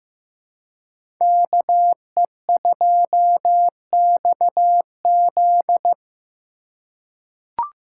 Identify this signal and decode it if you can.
Morse code, heard as a keyed tone, about 15 wpm, 700 Hz. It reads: KE2XZ
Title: USA Call Signs - 5 Characters - 2x Character Spacing - 15wpm